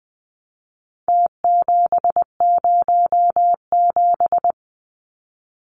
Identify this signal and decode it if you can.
Morse code, heard as a keyed tone, about 20 wpm, 700 Hz. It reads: T707
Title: Numbers - Contest Style - 4-Digit 20wpm